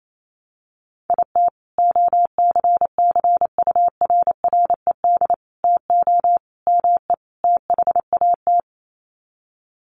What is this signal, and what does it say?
Morse code, heard as a keyed tone, about 28 wpm, 700 Hz. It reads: IT OCCURRED TO ME THAT
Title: Sets of 5 Words Encore 28wpm